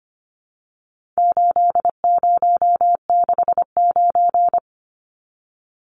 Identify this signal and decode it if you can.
Morse code, heard as a keyed tone, about 25 wpm, 700 Hz. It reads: 8069
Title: Numbers - 4-Digit 25wpm